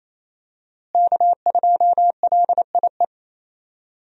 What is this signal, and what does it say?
Morse code, heard as a keyed tone, about 27 wpm, 700 Hz. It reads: K2LIE
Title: USA Call Signs - 5 Characters - Repeated 2x - 28wpm